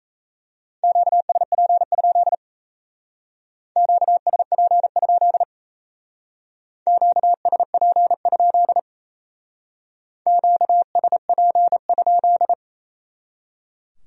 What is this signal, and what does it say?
Morse code, heard as a keyed tone, about 36 wpm, 700 Hz. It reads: QSP? QSP? QSP? QSP?